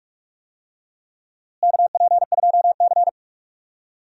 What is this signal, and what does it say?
Morse code, heard as a keyed tone, about 42 wpm, 700 Hz. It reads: KP2C
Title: DXCC-100 Call Signs - 4 Characters - Repeated 3x - 45wpm